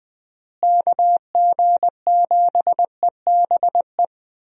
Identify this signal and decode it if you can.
Morse code, heard as a keyed tone, about 20 wpm, 700 Hz. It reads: KG7EBE
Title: USA Call Signs - 6 Characters - Repeated 3x - 20wpm